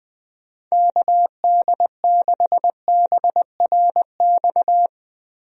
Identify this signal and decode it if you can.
Morse code, heard as a keyed tone, about 20 wpm, 700 Hz. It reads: KD6BRX
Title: USA Call Signs - 6 Characters - Repeated 3x - 20wpm